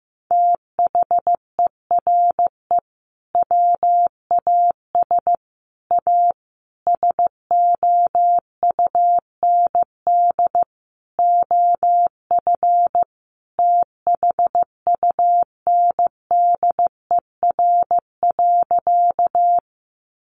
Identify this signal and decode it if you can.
Morse code, heard as a keyed tone, about 15 wpm, 700 Hz. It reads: THERE WAS A SOUND OF THUNDER.
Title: A Sound Of Thunder 15wpm